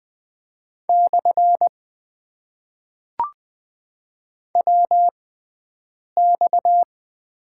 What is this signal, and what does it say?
Morse code, heard as a keyed tone, about 20 wpm, spaced as for 7 wpm, 700 Hz. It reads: / WX
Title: Top 90 QSO Elements - 6x Character Spacing - 20wpm